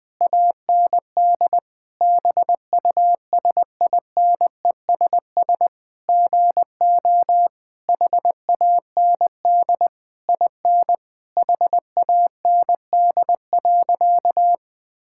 Morse code, heard as a keyed tone, about 20 wpm, 700 Hz. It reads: AND BUSINESS GO HAND IN HAND.
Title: Sentences from Top 1000 Words - Review Part 5 - 20wpm